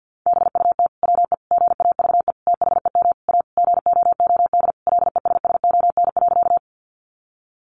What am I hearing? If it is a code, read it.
Morse code, heard as a keyed tone, about 50 wpm, 700 Hz. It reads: BUT WE GAVE THEM A GOOD LESSON.
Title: Sentences from Top 1200 Words 50wpm